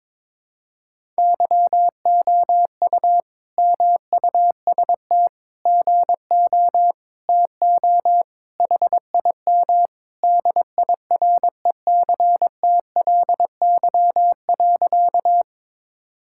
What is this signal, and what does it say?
Morse code, heard as a keyed tone, about 22 wpm, 700 Hz. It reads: YOU MUST GO TO HIM DIRECTLY.
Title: Sentences from Top 1000 Words - Review Part 5 - 22wpm